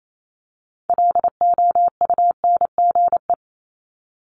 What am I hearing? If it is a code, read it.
Morse code, heard as a keyed tone, about 28 wpm, 700 Hz. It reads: LOUNGE
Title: Multi-letter Phonemes 28wpm